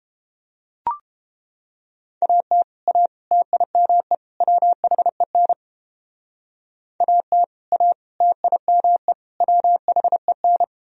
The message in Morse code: AT A TIME WHEN AT A TIME WHEN

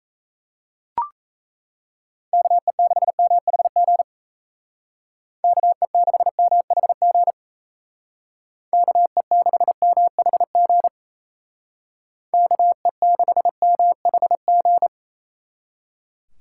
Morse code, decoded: KE6MHG KE6MHG KE6MHG KE6MHG

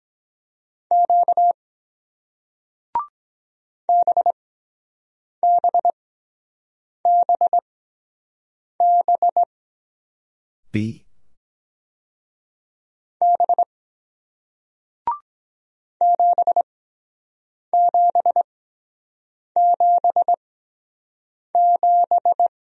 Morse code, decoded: Q BBBB B 7777

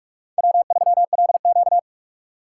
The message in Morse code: W3LQ

